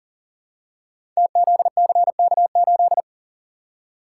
Morse code, heard as a keyed tone, about 41 wpm, 690 Hz. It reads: TZCK8
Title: Code Groups - 5 Letter:Numbers - 40wpm